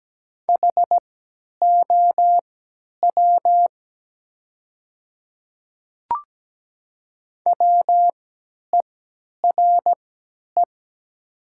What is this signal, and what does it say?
Morse code, heard as a keyed tone, about 17 wpm, spaced as for 10 wpm, 700 Hz. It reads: HOW WERE
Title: 3x Character Spacing - Top 100 Words - 17wpm